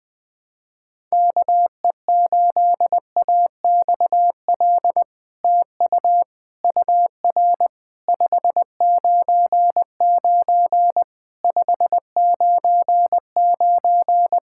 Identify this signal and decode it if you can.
Morse code, heard as a keyed tone, about 20 wpm, 700 Hz. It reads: KE8AXL TU UR 599 599
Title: SOTA QSO Practice 20wpm